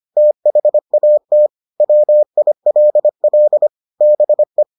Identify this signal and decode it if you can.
Morse code, heard as a keyed tone, about 25 wpm, 600 Hz. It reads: THAT WILL BE